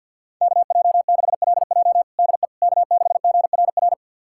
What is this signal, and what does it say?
Morse code, heard as a keyed tone, about 53 wpm, 700 Hz. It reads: KJ6LJ DE K6GRR
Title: USA Callsign DE Callsign - 5 Characters - 50wpm